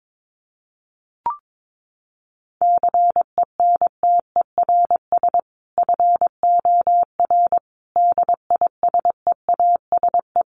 Morse code, heard as a keyed tone, about 22 wpm, 700 Hz. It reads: CENTERS FOR DISEASE
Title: Sets of 5 Words Encore2 22wpm